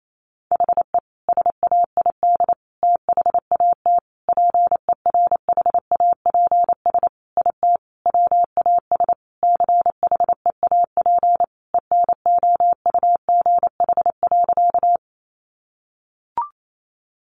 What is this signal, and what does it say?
Morse code, heard as a keyed tone, about 28 wpm, 700 Hz. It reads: HE SAID THAT PERHAPS IT WAS CHEAP ENOUGH.
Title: Sentences from Top 1200 Words 28wpm